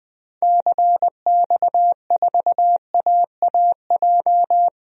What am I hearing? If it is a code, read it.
Morse code, heard as a keyed tone, about 20 wpm, 700 Hz. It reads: CX4AAJ
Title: DXCC-100 Call Signs - 6 Characters - Repeated 3x - 20wpm